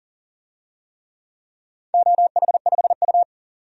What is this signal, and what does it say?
Morse code, heard as a keyed tone, about 39 wpm, 700 Hz. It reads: OH5U
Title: DXCC-100 Call Signs - 4 Characters - 40wpm